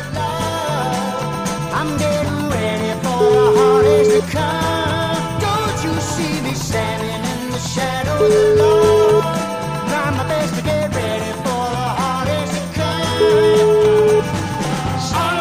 Ringback tone (Europe, 425 Hz), with noise.